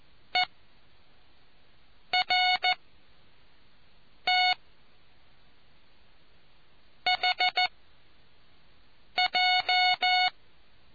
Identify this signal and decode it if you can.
Morse code, heard as a keyed tone, about 14 wpm, 730 Hz. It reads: E R T H J